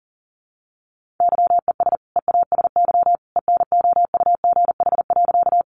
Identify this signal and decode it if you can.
Morse code, heard as a keyed tone, about 40 wpm, 700 Hz. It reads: YES EASY ENOUGH.
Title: Sentences from Top 700 Words 40wpm